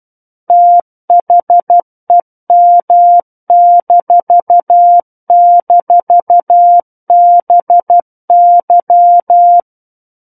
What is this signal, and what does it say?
Morse code, heard as a keyed tone, about 12 wpm, 700 Hz. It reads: THEM--BY
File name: Book One / 03-Chapter 3 - 12 WPM